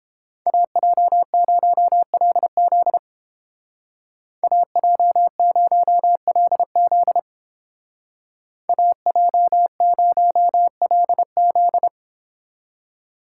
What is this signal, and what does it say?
Morse code, heard as a keyed tone, about 31 wpm, 700 Hz. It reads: AJ0LZ AJ0LZ AJ0LZ